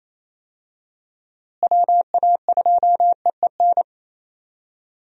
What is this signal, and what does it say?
Morse code, heard as a keyed tone, about 28 wpm, 700 Hz. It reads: WA2EEN
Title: USA Call Signs - 6 Characters - Repeated 3x - 28wpm